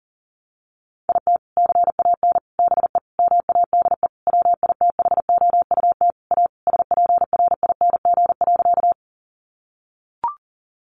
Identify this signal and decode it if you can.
Morse code, heard as a keyed tone, about 40 wpm, 700 Hz. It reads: IT CAN BE MADE WITHOUT A SPRING.
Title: Sentences from Top 1200 Words 40wpm